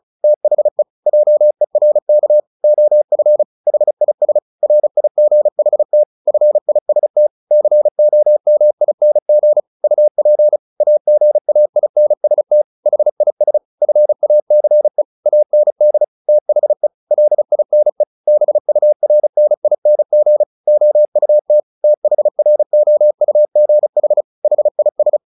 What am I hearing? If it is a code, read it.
Morse code, heard as a keyed tone, about 34 wpm, 600 Hz. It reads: THE JERK OF HIS RIGHT FIST COMING UP AGAINST HIS FACE AND THE LINE BURNING OUT THROUGH HIS
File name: the old man and the sea 2 35wpm